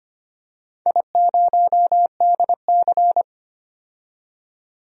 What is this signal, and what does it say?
Morse code, heard as a keyed tone, about 25 wpm, 700 Hz. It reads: I0DC
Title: DXCC-100 Call Signs - 4 Characters - 25wpm